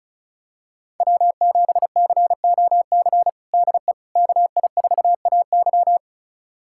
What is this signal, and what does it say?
Morse code, heard as a keyed tone, about 36 wpm, 700 Hz. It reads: W7COC DE KI4AY
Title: USA Callsign DE Callsign - 5 Characters - 35wpm